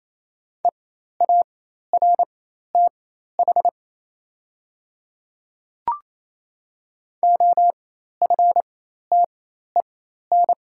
Morse code, heard as a keyed tone, about 28 wpm, spaced as for 13 wpm, 700 Hz. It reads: EARTH OFTEN
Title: 4x Character Spacing - Top 300 Words - 28wpm